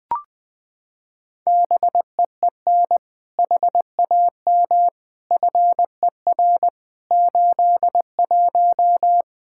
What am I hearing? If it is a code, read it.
Morse code, heard as a keyed tone, about 20 wpm, 700 Hz. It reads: BEEN HAM FER 81
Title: Practice QSO - Exchange 3 - More Info - 20wpm